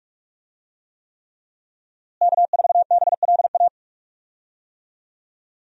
Morse code, heard as a keyed tone, about 46 wpm, 700 Hz. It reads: K4BLA
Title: USA Call Signs - 5 Characters - Repeated 3x - 45wpm